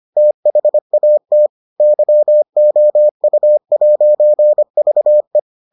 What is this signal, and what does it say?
Morse code, heard as a keyed tone, about 25 wpm, 600 Hz. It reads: THAT YOU'VE